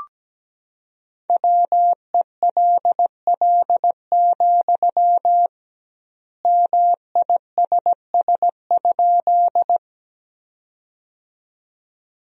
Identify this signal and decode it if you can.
Morse code, heard as a keyed tone, about 17 wpm, 700 Hz. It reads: WELL, MISS?
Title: 2x Word-Spacing - Sentences from Top 500 Words - 17wpm